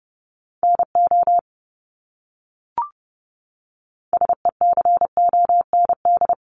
Morse code, heard as a keyed tone, about 30 wpm, 700 Hz. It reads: NO SECOND
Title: Top 500 Words Repeated 30wpm